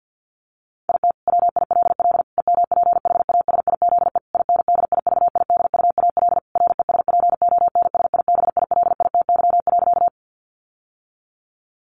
Morse code, heard as a keyed tone, about 50 wpm, 700 Hz. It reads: IT WILL EMPHASIZE INDIVIDUAL RESPONSIBILITY.
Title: Sentences from Top 1500 Words 50wpm